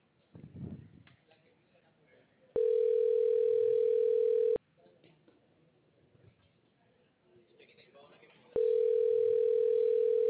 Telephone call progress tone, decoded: ringback tone

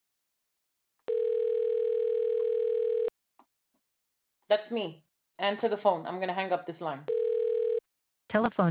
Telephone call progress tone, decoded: ringback tone

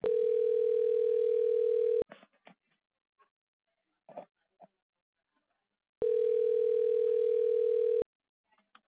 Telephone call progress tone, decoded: ringback tone

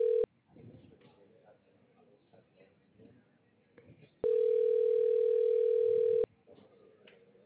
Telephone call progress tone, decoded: ringback tone

